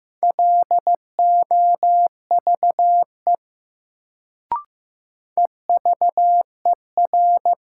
Morse code, heard as a keyed tone, about 15 wpm, 700 Hz. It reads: LOVE EVER
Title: Sentences from Top 400 Words - New Word Drill - 15wpm